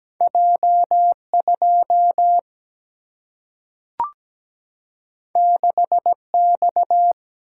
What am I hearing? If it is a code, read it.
Morse code, heard as a keyed tone, about 17 wpm, 700 Hz. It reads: J2 6X
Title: Single-Letter Single-Number 17wpm